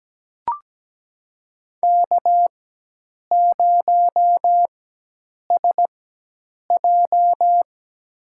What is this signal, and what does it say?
Morse code, heard as a keyed tone, about 17 wpm, spaced as for 8 wpm, 700 Hz. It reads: K0SJ
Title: USA Call Signs - 4 Characters - 4x Character Spacing - 17wpm